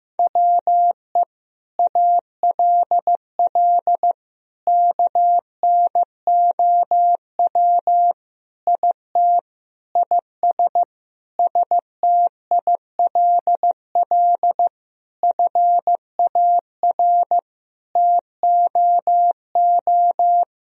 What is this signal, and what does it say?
Morse code, heard as a keyed tone, about 16 wpm, 700 Hz. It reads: WE ALL KNOW IT IS STILL FAR TOO
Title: Sentences from Top 1000 Words - Review Part 1 - 15wpm